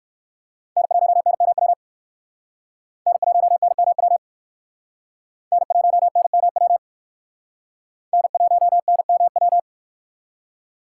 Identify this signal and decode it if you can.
Morse code, heard as a keyed tone, about 57 wpm, 700 Hz. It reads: N1NMW N1NMW N1NMW N1NMW